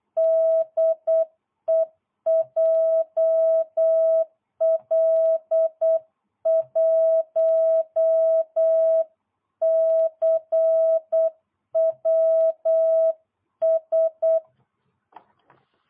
Morse code, heard as a keyed tone, about 8 wpm, 660 Hz. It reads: DE JL1CWS